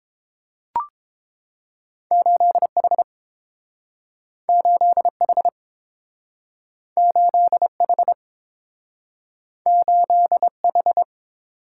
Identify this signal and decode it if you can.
Morse code, heard as a keyed tone, about 30 wpm, 700 Hz. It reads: 8H 8H 8H 8H